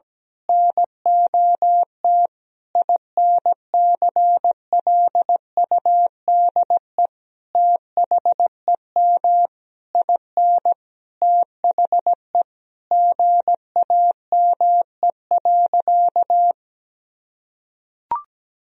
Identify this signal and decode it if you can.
Morse code, heard as a keyed tone, about 17 wpm, 700 Hz. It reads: NOT INCLUDE THEM IN THE GAME.
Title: Sentences from Top 1000 Words - Review Part 3 - 17wpm